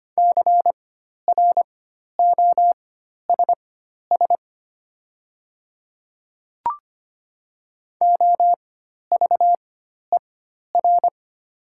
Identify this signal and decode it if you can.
Morse code, heard as a keyed tone, about 25 wpm, spaced as for 12 wpm, 690 Hz. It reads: CROSS OVER
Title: 4x Character Spacing - Top 500 Words - 25wpm